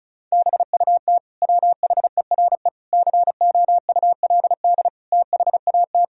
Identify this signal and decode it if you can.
Morse code, heard as a keyed tone, about 36 wpm, 700 Hz. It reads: BUT WHERE COULD THAT
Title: Sentences from Top 1000 Words - Review Part 2 - 35wpm